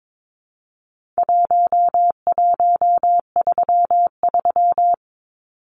Morse code, heard as a keyed tone, about 22 wpm, 700 Hz. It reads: 1133
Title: Numbers - 4-Digit 22wpm